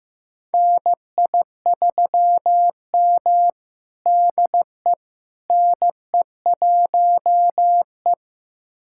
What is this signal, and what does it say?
Morse code, heard as a keyed tone, about 15 wpm, 700 Hz. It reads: NI3M DE NE1E
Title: USA Callsign DE Callsign - 4 Characters - 15wpm